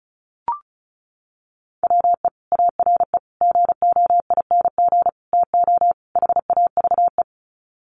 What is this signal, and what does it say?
Morse code, heard as a keyed tone, about 34 wpm, 700 Hz. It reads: WE ARE GOING TO HAVE
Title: Sets of 5 Words Encore 35wpm